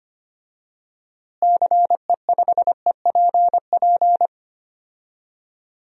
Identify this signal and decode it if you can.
Morse code, heard as a keyed tone, about 25 wpm, 700 Hz. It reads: CE5EPP
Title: DXCC-100 Call Signs - 6 Characters - Repeated 3x - 25wpm